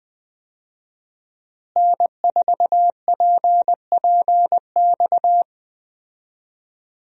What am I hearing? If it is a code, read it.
Morse code, heard as a keyed tone, about 20 wpm, 700 Hz. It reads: N4PPX
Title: USA Call Signs - 5 Characters - Repeated 2x - 20wpm